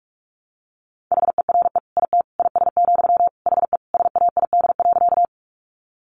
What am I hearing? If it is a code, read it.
Morse code, heard as a keyed tone, about 44 wpm, 700 Hz. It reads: HERE IT IS, HE SAID.
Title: Sentences from Top 200 Words 45wpm